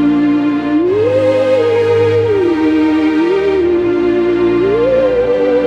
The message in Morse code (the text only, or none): none